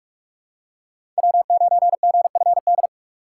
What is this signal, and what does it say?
Morse code, heard as a keyed tone, about 42 wpm, 700 Hz. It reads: W9GFD
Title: USA Call Signs - 5 Characters - 4x Character Spacing - 45wpm